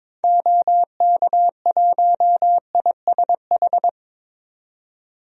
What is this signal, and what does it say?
Morse code, heard as a keyed tone, about 22 wpm, 700 Hz. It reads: OK1ISH
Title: DXCC-100 Call Signs - 6 Characters - 22wpm